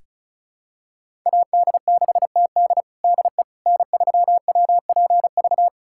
Morse code, heard as a keyed tone, about 34 wpm, 700 Hz. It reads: AD6TD DE N3WPV
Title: USA Callsign DE Callsign - 5 Characters - 35wpm